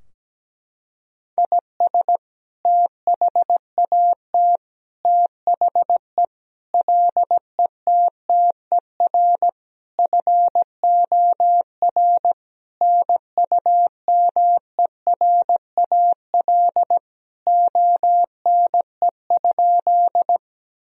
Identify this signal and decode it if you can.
Morse code, heard as a keyed tone, about 17 wpm, 700 Hz. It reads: IS THAT THE LETTER FOR NUMERAL ONE?
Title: Sentences from Top 1000 Words - Review Part 2 - 17wpm